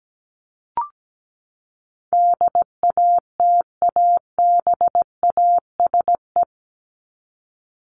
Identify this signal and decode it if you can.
Morse code, heard as a keyed tone, about 17 wpm, 700 Hz. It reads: DATABASE
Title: Top 1751-2000 Words 17wpm